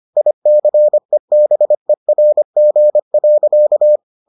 Morse code, heard as a keyed tone, about 25 wpm, 600 Hz. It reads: ICEBERG.